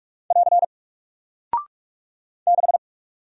Morse code, heard as a keyed tone, about 45 wpm, 700 Hz. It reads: <AR> 6
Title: Single Character (Letter:Number:Punctuation:Prosign) Alt 45wpm